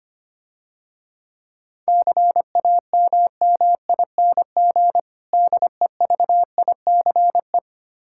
Morse code, heard as a keyed tone, about 25 wpm, 700 Hz. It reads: CAMMING DEVICE
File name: practice / 25wpm / Difficult Words - Speed-Racing with Hints - 25wpm